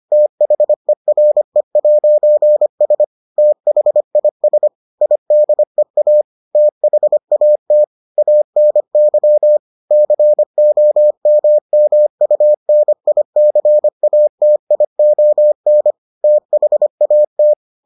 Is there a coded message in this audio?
Yes: Morse code from 0.1 to 17.5 s, heard as a keyed tone, about 25 wpm, 600 Hz. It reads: THERE'S THIS IDEA THAT ANY COMMUNICATION THAT